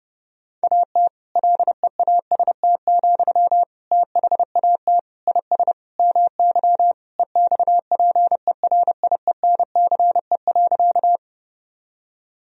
Morse code, heard as a keyed tone, about 32 wpm, 700 Hz. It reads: AT LEAST, THAT IS MY EXPERIENCE.
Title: Sentences from Top 1000 Words - Review Part 3 - 30wpm